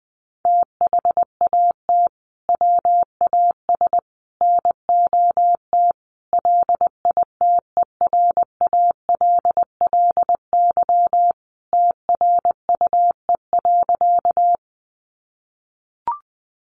Morse code, heard as a keyed tone, about 20 wpm, 700 Hz. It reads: THAT WAS NOT LITERALLY TRUE.
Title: Sentences from Top 1400 Words 20wpm